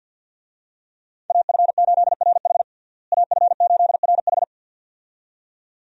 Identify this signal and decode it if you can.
Morse code, heard as a keyed tone, about 48 wpm, 700 Hz. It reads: AF8RH AF8RH
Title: USA Call Signs - 5 Characters - Repeated 2x - 50wpm